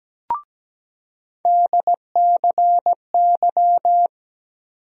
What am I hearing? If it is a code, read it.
Morse code, heard as a keyed tone, about 17 wpm, 700 Hz. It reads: DCY
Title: USA Call Sign Postfixes - 3 Letters 17wpm